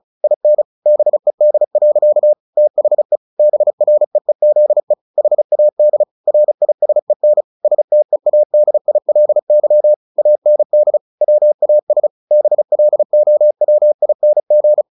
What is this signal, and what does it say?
Morse code, heard as a keyed tone, about 34 wpm, 600 Hz. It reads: IN BED. THE BREEZE HAD RISEN STEADILY AND WAS BLOWING
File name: the old man and the sea 2 35wpm